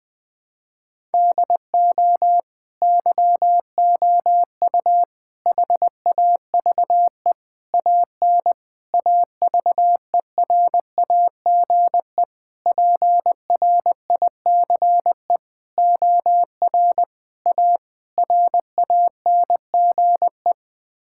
Morse code, heard as a keyed tone, about 20 wpm, 700 Hz. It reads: DO YOU HAVE AN AVERAGE PRICE OR A RANGE
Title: Sets of 9 Words - 20wpm